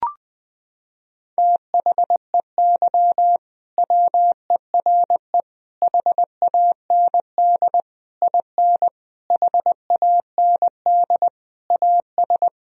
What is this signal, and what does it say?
Morse code, heard as a keyed tone, about 20 wpm, 700 Hz. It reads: THEY WERE HAND IN HAND AS
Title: Sentences from Top 1000 Words - Review Part 2 - 20wpm